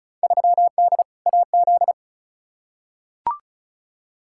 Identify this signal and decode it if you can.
Morse code, heard as a keyed tone, about 34 wpm, 700 Hz. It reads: 3D AZ
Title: Field Day Exchanges - 35wpm